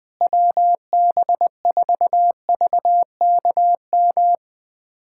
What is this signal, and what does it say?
Morse code, heard as a keyed tone, about 21 wpm, 700 Hz. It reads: WB4VKM
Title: USA Callsign DE Callsign - 6 Characters - 20wpm